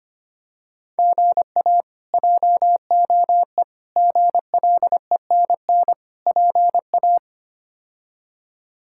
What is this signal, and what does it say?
Morse code, heard as a keyed tone, about 25 wpm, 700 Hz. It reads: GA JOE GLENN PA
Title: K1USN SST QSO Practice - 25wpm